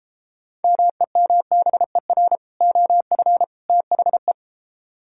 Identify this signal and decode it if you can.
Morse code, heard as a keyed tone, about 33 wpm, 700 Hz. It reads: MEMBER OF THE